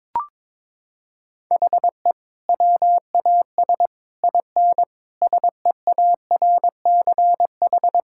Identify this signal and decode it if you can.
Morse code, heard as a keyed tone, about 22 wpm, 700 Hz. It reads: HE WAS IN SEARCH
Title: Sentences from Top 1000 Words - Review Part 5 - 22wpm